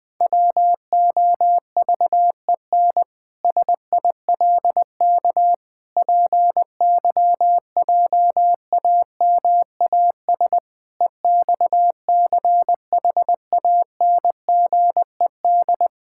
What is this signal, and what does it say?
Morse code, heard as a keyed tone, about 21 wpm, 700 Hz. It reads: WOVEN SILK PYJAMAS EXCHANGED
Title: Pangram Challenge v2 - 20wpm